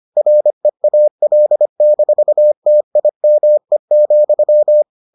Morse code, heard as a keyed tone, about 25 wpm, 600 Hz. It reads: REAL-TIME,